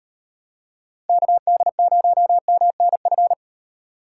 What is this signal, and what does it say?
Morse code, heard as a keyed tone, about 38 wpm, 700 Hz. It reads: KD0MNF